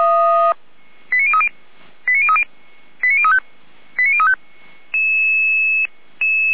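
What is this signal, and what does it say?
Five-tone selective call (ZVEI), heard as quick runs of five tones, tones of about 70 ms. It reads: 89030 89030 89035 89035
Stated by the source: ZVEI five-tone call received on SDR.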